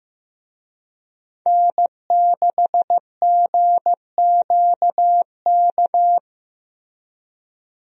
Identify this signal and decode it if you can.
Morse code, heard as a keyed tone, about 15 wpm, 700 Hz. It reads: N6GQK
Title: USA Call Signs - 5 Characters - Repeated 2x - 15wpm